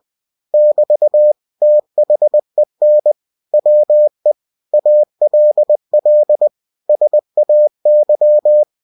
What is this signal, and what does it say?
Morse code, heard as a keyed tone, about 20 wpm, 600 Hz. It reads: <BT> THEN WE ALL SAY
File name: cw-books / winnie-the-pooh / winnie-20wpm-page-0064